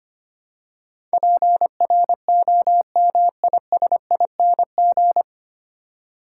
Morse code, heard as a keyed tone, about 25 wpm, 700 Hz. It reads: PROMISING
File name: Nine Letter Words - 25wpm